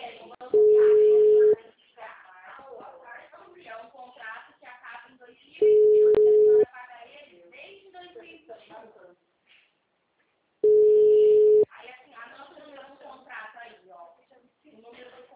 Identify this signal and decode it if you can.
Ringback tone (Europe, 425 Hz), with speech.
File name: out- -2024- - - .64678